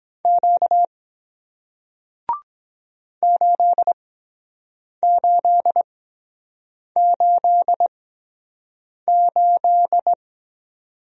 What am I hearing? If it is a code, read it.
Morse code, heard as a keyed tone, about 23 wpm, spaced as for 7 wpm, 700 Hz. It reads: Q 8888